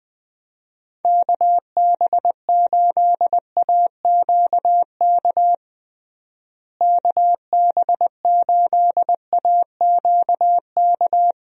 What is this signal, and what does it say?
Morse code, heard as a keyed tone, about 20 wpm, 700 Hz. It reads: KB8AQK KB8AQK
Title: USA Call Signs - 6 Characters - Repeated 3x - 20wpm